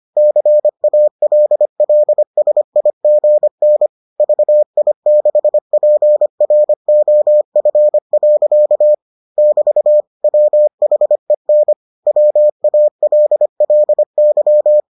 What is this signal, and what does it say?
Morse code, heard as a keyed tone, about 25 wpm, 600 Hz. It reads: CALLSIGN VI6PROF. <BT> WHEN WALLY